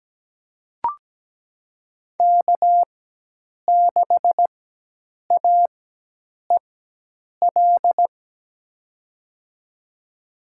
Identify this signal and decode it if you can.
Morse code, heard as a keyed tone, about 17 wpm, spaced as for 8 wpm, 700 Hz. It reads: K6AEL
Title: USA Call Signs - 5 Characters - 4x Character Spacing - 17wpm